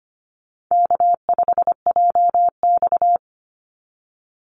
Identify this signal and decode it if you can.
Morse code, heard as a keyed tone, about 25 wpm, 700 Hz. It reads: K5JX
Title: Easy Call Signs - 25wpm